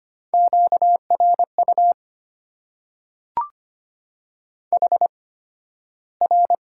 Morse code, heard as a keyed tone, about 25 wpm, spaced as for 13 wpm, 700 Hz. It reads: QRU HR
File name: Top 90 QSO Elements - 8x Character Spacing - 25wpm